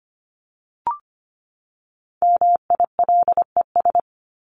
Morse code, heard as a keyed tone, about 25 wpm, 700 Hz. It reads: MILES
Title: ES:S Suffix - Top 100 Words with Base Words 25wpm